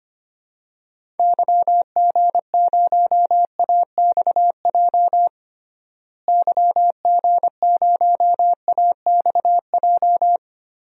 Morse code, heard as a keyed tone, about 25 wpm, 690 Hz. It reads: YG0AXJ YG0AXJ
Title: DXCC-100 Call Signs - 6 Characters - Repeated 3x - 25wpm